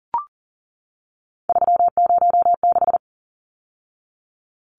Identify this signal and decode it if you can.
Morse code, heard as a keyed tone, about 40 wpm, 700 Hz. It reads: 306
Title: Numbers - 3-Digit 40wpm